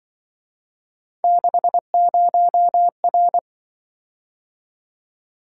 Morse code, heard as a keyed tone, about 24 wpm, 700 Hz. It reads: 60R
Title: Speed-Racing - Three Letter-Number Combinations - 20wpm